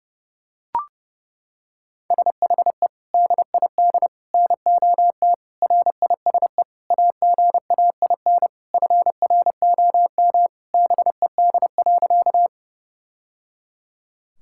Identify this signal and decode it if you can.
Morse code, heard as a keyed tone, about 32 wpm, 700 Hz. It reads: SHE DID NOT RISE AGAIN FROM BED.
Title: Sentences from Top 1000 Words - Review Part 6 - 30wpm